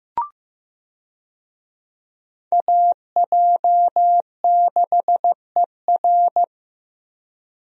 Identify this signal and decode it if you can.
Morse code, heard as a keyed tone, about 15 wpm, 700 Hz. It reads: AJ6ER
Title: USA Call Signs - 5 Characters - Repeated 3x - 15wpm